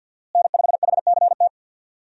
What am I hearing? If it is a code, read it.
Morse code, heard as a keyed tone, about 53 wpm, 700 Hz. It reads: N5HCT
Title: USA Call Signs - 5 Characters - Repeated 3x - 50wpm